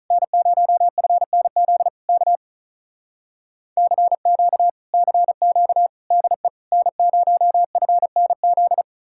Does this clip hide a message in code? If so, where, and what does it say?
Yes: Morse code from 0.1 to 8.8 s, heard as a keyed tone, about 46 wpm, 700 Hz. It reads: N0FNZ K CQ CQ DE N0FNZ